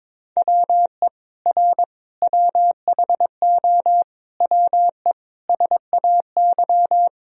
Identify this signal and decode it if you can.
Morse code, heard as a keyed tone, about 21 wpm, 700 Hz. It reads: WE R WHO WE SAY
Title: Sentences from Top 200 Words and Abbreviations 22wpm